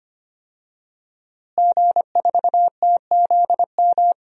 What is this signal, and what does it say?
Morse code, heard as a keyed tone, about 25 wpm, 700 Hz. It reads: G4TZM
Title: DXCC-100 Call Signs - 5 Characters - Repeated 2x - 25wpm